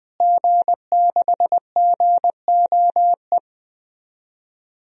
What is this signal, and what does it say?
Morse code, heard as a keyed tone, about 20 wpm, 700 Hz. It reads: G6GOE
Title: DXCC-100 Call Signs - 5 Characters - Repeated 2x - 20wpm